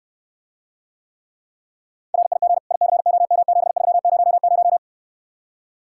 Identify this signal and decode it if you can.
Morse code, heard as a keyed tone, about 68 wpm, 700 Hz. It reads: RIG ICOM7300